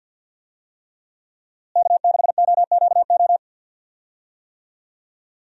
Morse code, heard as a keyed tone, about 52 wpm, 700 Hz. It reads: K6OQO
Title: USA Call Signs - 5 Characters - Repeated 3x - 50wpm